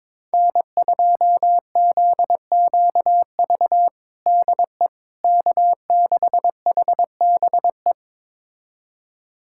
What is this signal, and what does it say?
Morse code, heard as a keyed tone, about 22 wpm, 700 Hz. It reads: N2ZQV DE K6HBE
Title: USA Callsign DE Callsign 22wpm